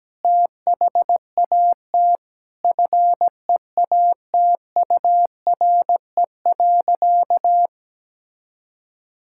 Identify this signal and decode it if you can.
Morse code, heard as a keyed tone, about 17 wpm, 700 Hz. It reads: THAT FEATURE.
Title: Sentences from Top 1000 Words - Review Part 5 - 17wpm